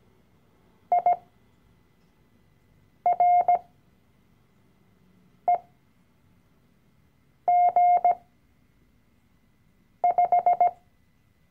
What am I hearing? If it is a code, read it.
Morse code, heard as a keyed tone, about 17 wpm, spaced as for 4 wpm, 710 Hz. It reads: IREG5
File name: CW-Online-Kurs / Anfänger / Lektion 012